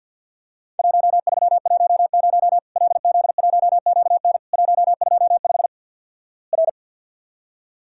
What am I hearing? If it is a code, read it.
Morse code, heard as a keyed tone, about 50 wpm, 700 Hz. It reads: 1310 LZ1QN 125 R
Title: Worked All Europe DX Contest - QTC Exchange - 50wpm